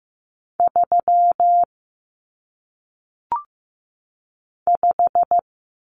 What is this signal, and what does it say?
Morse code, heard as a keyed tone, about 15 wpm, 700 Hz. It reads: 3 5
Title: Common Cut Numbers - 1-Digit 15wpm